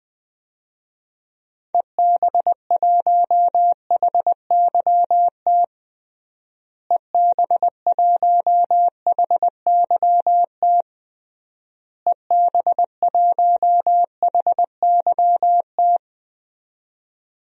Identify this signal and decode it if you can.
Morse code, heard as a keyed tone, about 20 wpm, 700 Hz. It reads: EB1HYT EB1HYT EB1HYT
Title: DXCC-100 Call Signs - 6 Characters - Repeated 3x - 20wpm